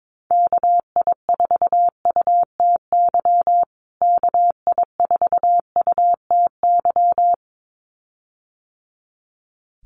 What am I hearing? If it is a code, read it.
Morse code, heard as a keyed tone, about 22 wpm, 700 Hz. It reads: KI4UTY KI4UTY
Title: USA Call Signs Repeated 2x 22wpm